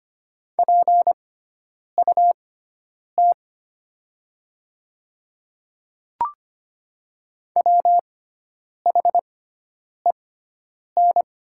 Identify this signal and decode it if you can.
Morse code, heard as a keyed tone, about 25 wpm, spaced as for 9 wpm, 700 Hz. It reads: PUT WHEN
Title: 6x Character Spacing - Top 200 Words - 25wpm